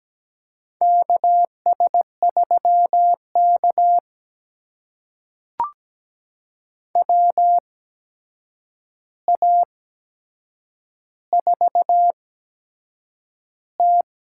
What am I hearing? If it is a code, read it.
Morse code, heard as a keyed tone, about 17 wpm, spaced as for 7 wpm, 700 Hz. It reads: KS3K WA4T
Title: USA Call Signs - 8x Character Spacing - 17wpm